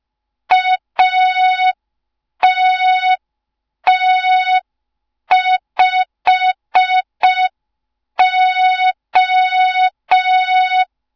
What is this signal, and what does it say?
Morse code, heard as a keyed tone, about 5 wpm, 3700 Hz. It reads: ATT5O
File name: Morze-HA2MN / hang / 07 GYAKORLAT